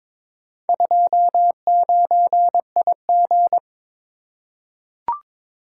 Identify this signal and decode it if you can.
Morse code, heard as a keyed tone, about 22 wpm, 700 Hz. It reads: 29IG
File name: practice / 22wpm / Four Letter-Number Combinations - 22wpm